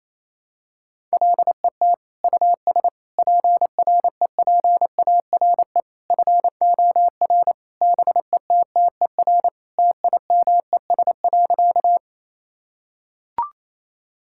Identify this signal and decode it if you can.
Morse code, heard as a keyed tone, about 28 wpm, 700 Hz. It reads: LET US PREPARE FOR BETTER TIMES.
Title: Sentences from Top 1000 Words - Review Part 6 - 28wpm